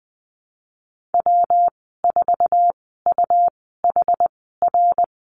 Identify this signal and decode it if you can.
Morse code, heard as a keyed tone, about 20 wpm, spaced as for 14 wpm, 700 Hz. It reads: W4UHR
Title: 193 Lesson 31g - Letter v - Call Signs 20wpm